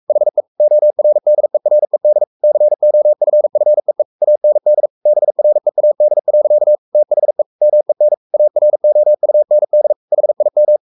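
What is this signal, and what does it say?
Morse code, heard as a keyed tone, about 44 wpm, 600 Hz. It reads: HE ORDERED COFFEE AND BREAD. THE MEN AROUND HIM
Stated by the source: Morse code at 43 wpm sending Ken Follett's novel Never in English, from CW med Gnister.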